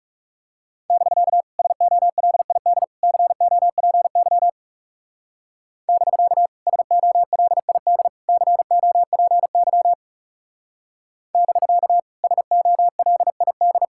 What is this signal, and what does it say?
Morse code, heard as a keyed tone, about 35 wpm, 700 Hz. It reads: <BK> SOLID COPY <BK> SOLID COPY <BK> SOLID
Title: Speed-Racing Common QSO Phrases - 30wpm